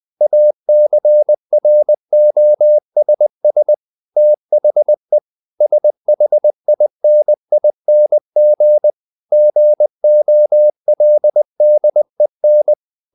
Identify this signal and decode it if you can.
Morse code, heard as a keyed tone, about 20 wpm, 600 Hz. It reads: ACROSS THE SHINING GOLDEN
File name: the old man and the sea part 2 20wpm